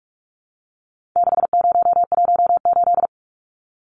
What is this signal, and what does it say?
Morse code, heard as a keyed tone, about 44 wpm, 700 Hz. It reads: 6018
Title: Numbers - 4-Digit 45wpm